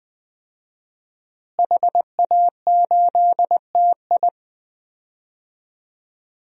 Morse code, heard as a keyed tone, about 21 wpm, 700 Hz. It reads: HA8TI
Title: DXCC-100 Call Signs - 5 Characters - Repeated 3x - 20wpm